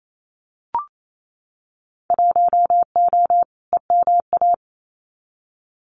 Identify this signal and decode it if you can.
Morse code, heard as a keyed tone, about 28 wpm, 700 Hz. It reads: 1O EMA
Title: Winter Field Day Exchanges 28wpm